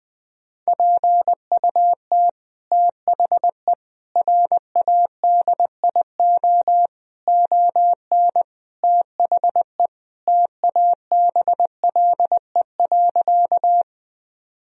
Morse code, heard as a keyed tone, about 20 wpm, 700 Hz. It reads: PUT THE RADIO ON THE TABLE.
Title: Sentences from Top 1000 Words - Review Part 3 - 20wpm